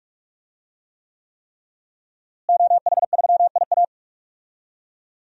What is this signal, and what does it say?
Morse code, heard as a keyed tone, about 46 wpm, 700 Hz. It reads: OH3IA
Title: DXCC-100 Call Signs - 5 Characters - Repeated 3x - 45wpm